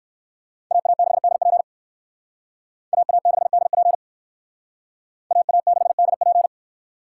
Morse code, heard as a keyed tone, about 57 wpm, 700 Hz. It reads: AA6DP AA6DP AA6DP